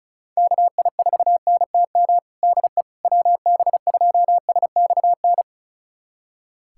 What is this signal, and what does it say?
Morse code, heard as a keyed tone, about 36 wpm, 700 Hz. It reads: KI4NTM DE WB2SXN
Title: USA Callsign DE Callsign 35wpm